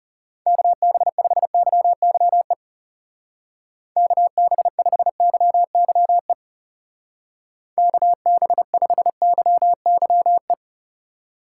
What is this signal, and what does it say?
Morse code, heard as a keyed tone, about 39 wpm, 700 Hz. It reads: KB5YYE KB5YYE KB5YYE